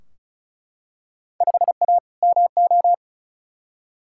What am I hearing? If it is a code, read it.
Morse code, heard as a keyed tone, about 36 wpm, 690 Hz. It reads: 5A MO
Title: Field Day Exchanges Encore - 35wpm